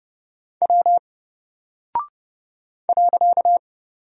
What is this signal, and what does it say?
Morse code, heard as a keyed tone, about 30 wpm, 700 Hz. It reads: W .
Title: Single Character (Letter:Number:Punctuation:Prosign) Alt 30wpm